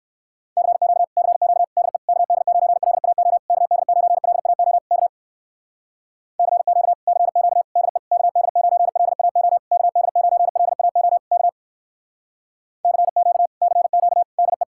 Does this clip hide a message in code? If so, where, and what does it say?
Yes: Morse code from 0.6 to 14.6 s, heard as a keyed tone, about 63 wpm, 700 Hz. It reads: CQ CQ DE KD9LAO KD9LAO K CQ CQ DE KD9LAO KD9LAO K CQ CQ DE